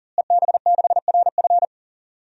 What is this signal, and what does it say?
Morse code, heard as a keyed tone, about 40 wpm, 700 Hz. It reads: EB6RF
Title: DXCC-100 Call Signs - 5 Characters - Repeated 3x - 40wpm